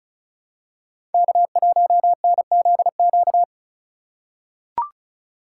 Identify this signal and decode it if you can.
Morse code, heard as a keyed tone, about 36 wpm, 700 Hz. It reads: K1NZQ
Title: USA Call Signs - 3x Character Spacing - 35wpm